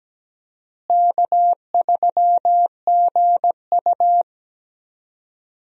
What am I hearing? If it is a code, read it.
Morse code, heard as a keyed tone, about 17 wpm, 700 Hz. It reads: K3GU
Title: USA Call Signs - 4 Characters - Repeated 3x - 17wpm